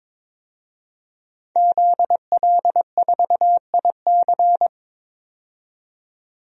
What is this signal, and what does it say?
Morse code, heard as a keyed tone, about 22 wpm, 700 Hz. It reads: ZL4IC
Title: DXCC-100 Call Signs - 5 Characters - 22wpm